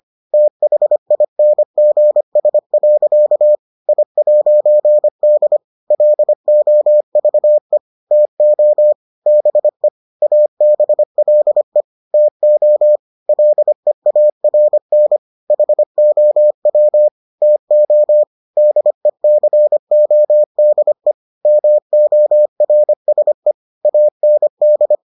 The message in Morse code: THINGS. I'D LOVE TO BE ABLE TO LEARN HOW TO DECODE MORSE AND

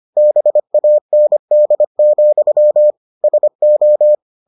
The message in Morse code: BAND, SO

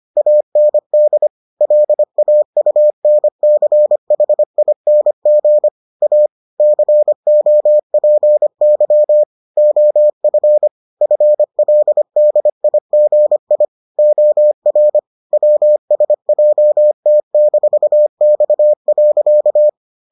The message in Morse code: AND LAUNCHING A COPY OF FLDIGI OR WSJT-X.

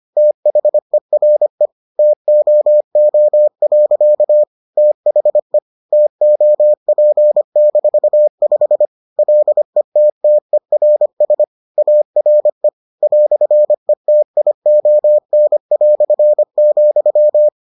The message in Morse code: THERE TOO. THE TOP-5 LETTERS ARE "ETION",

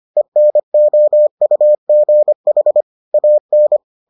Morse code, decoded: ENOUGH AN

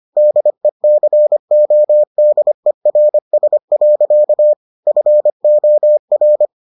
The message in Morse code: DECODERS. FOR

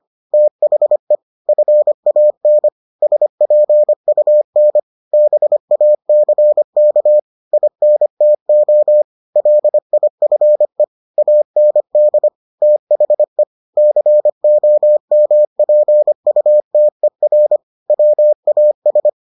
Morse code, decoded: THE FAN SPUN BACK INTO LIFE AND THE COMPUTER WAS

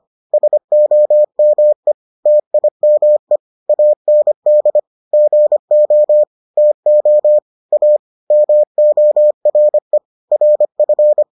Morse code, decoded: SOME TIME AND GO TO A MORE RF

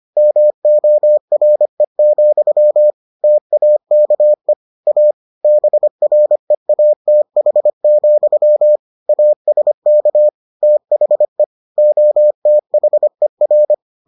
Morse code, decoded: MORE, TAKE A BREATH, ASK THE OTHER

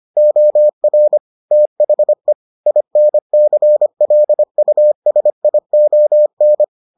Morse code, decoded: OR THE INCLUSION

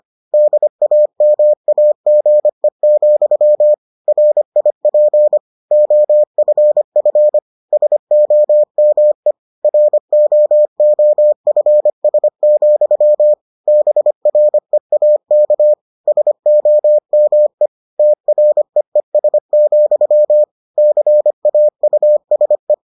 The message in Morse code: DAMAGE, RIP OFF SOME ROOFS, BREAK SOME TREES, CAUSE

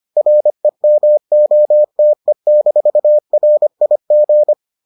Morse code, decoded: REMOTE-RIG

